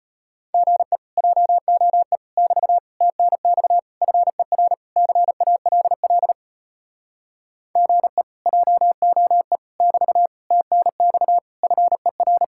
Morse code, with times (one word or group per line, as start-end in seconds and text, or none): 0.5-1.0 GE
1.2-2.2 JOE
2.4-2.8 <BT>
3.0-3.8 TNX
4.0-4.7 FER
5.0-6.3 CALL
7.8-8.2 GE
8.5-9.6 JOE
9.8-10.3 <BT>
10.5-11.4 TNX
11.6-12.4 FER